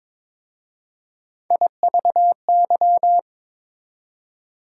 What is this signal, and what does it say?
Morse code, heard as a keyed tone, about 22 wpm, 690 Hz. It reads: IVY
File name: Difficult Words with Hints - 22wpm